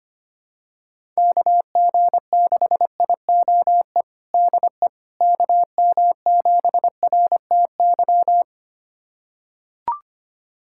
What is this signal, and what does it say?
Morse code, heard as a keyed tone, about 25 wpm, 700 Hz. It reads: KG6IOE DE KM7RTY
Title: USA Callsign DE Callsign 25wpm